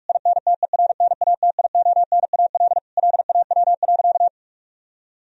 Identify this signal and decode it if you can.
Morse code, heard as a keyed tone, about 42 wpm, 700 Hz. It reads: INTERNATIONAL LAW.
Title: Sentences from Top 1000 Words - Review Part 4 - 45wpm